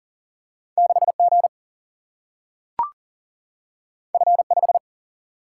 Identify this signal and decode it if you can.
Morse code, heard as a keyed tone, about 42 wpm, 700 Hz. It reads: 6 G F 5
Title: Single-Letter Single-Number 40wpm